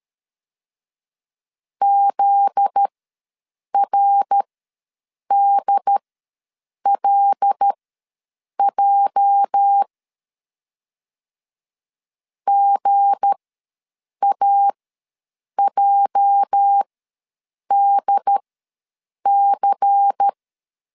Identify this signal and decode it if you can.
Morse code, heard as a keyed tone, about 13 wpm, spaced as for 7 wpm, 800 Hz. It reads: ZRDLJ GAJDC
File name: LEZIONI / 8 YJ45 / 8 YJ ws 12wpm 3342